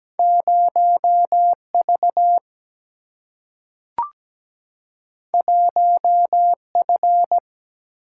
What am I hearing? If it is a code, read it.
Morse code, heard as a keyed tone, about 17 wpm, 700 Hz. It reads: 0 V 1 F
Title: Single-Letter Single-Number 17wpm